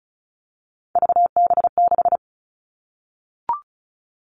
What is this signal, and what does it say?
Morse code, heard as a keyed tone, about 34 wpm, 700 Hz. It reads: VB6
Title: All Cut Numbers - 3-Digit 35wpm